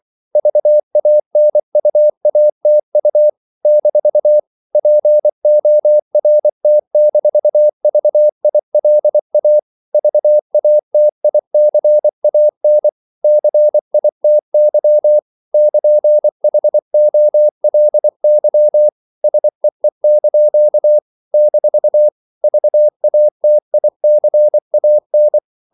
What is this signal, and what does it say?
Morse code, heard as a keyed tone, about 24 wpm, 600 Hz. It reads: VANUATU - PORT-VILA VATICAN CITY <KN>HOLY SEE) - VATICAN
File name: CW / Practice / NationalCapitals24